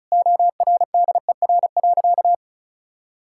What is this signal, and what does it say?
Morse code, heard as a keyed tone, about 36 wpm, 700 Hz. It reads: ORDER.
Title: Sentences from Top 1000 Words - Review Part 6 - 35wpm